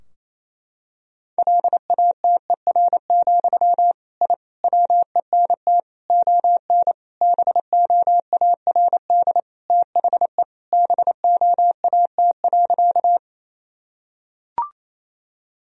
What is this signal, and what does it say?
Morse code, heard as a keyed tone, about 28 wpm, 700 Hz. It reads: LATER, I WENT ON BOARD THE BOAT.
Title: Sentences from Top 1000 Words - Review Part 4 - 28wpm